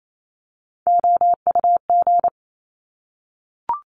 Morse code, heard as a keyed tone, about 28 wpm, 700 Hz. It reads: OUG
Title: Top Three Letter Combinations in Words 28wpm